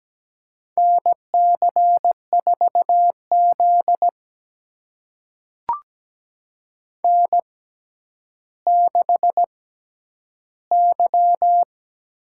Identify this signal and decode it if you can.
Morse code, heard as a keyed tone, about 17 wpm, spaced as for 10 wpm, 700 Hz. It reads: NC4Z N6Y
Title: USA Call Signs - 4 Characters - 6x Character Spacing - 17wpm